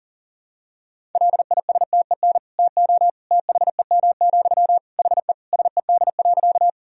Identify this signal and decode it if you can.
Morse code, heard as a keyed tone, about 37 wpm, 700 Hz. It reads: LISTEN TO THEM, HE SED.
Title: Sentences from Top 400 Words and Abbreviations 40wpm